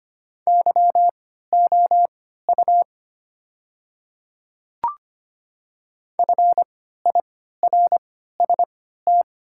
Morse code, heard as a keyed tone, about 25 wpm, spaced as for 14 wpm, 700 Hz. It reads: YOU FIRST
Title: 3x Character Spacing - Top 100 Words - 25wpm